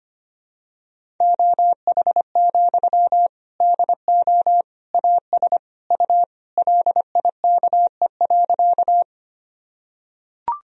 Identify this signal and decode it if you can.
Morse code, heard as a keyed tone, about 25 wpm, 700 Hz. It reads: OH, DO AS U LIKE.
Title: Sentences from Top 500 Words and Abbreviations 25wpm